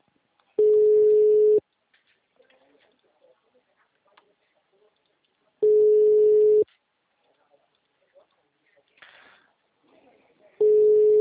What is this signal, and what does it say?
Ringback tone (Europe, 425 Hz), with speech.